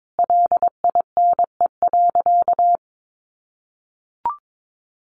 Morse code, heard as a keyed tone, about 22 wpm, 700 Hz. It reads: LINE.
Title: Sentences from Top 1200 Words 22wpm